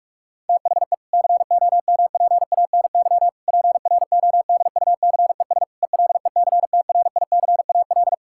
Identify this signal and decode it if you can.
Morse code, heard as a keyed tone, about 46 wpm, 700 Hz. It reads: THE COMPANY PRODUCES ELECTRICAL
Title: Sentences from Top 1000 Words - Review Part 6 - 45wpm